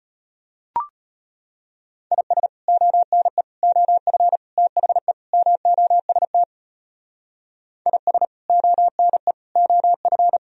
Morse code, heard as a keyed tone, about 32 wpm, 700 Hz. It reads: IS ONE OF THE MOST IS ONE OF